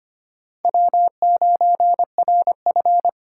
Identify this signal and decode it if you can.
Morse code, heard as a keyed tone, about 25 wpm, 700 Hz. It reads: W9RF
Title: USA Call Signs - 4 Characters - 3x Character Spacing - 25wpm